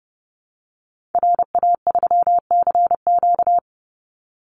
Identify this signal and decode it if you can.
Morse code, heard as a keyed tone, about 30 wpm, 700 Hz. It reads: RA3CQ
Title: Call Signs Repeated 2x - 30wpm